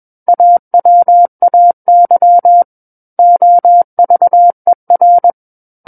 Morse code, heard as a keyed tone, about 21 wpm, 700 Hz. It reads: AWAY OVER
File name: Book One / 12-Chapter 12 - 21 WPM